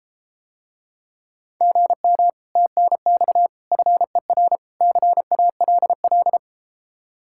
Morse code, heard as a keyed tone, about 33 wpm, 700 Hz. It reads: GM TNX FER CALL